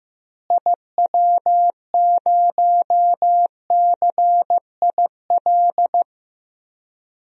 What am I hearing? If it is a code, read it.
Morse code, heard as a keyed tone, about 15 wpm, 700 Hz. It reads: IW0CIL
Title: DXCC-100 Call Signs - 6 Characters - Repeated 2x - 15wpm